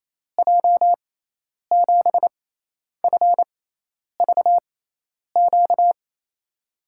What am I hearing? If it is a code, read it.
Morse code, heard as a keyed tone, about 28 wpm, spaced as for 10 wpm, 700 Hz. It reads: J7FVQ
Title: USA Call Signs - 6x Character Spacing - 28wpm